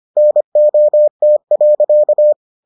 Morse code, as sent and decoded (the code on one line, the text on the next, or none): -. --- - .-.-.-
NOT.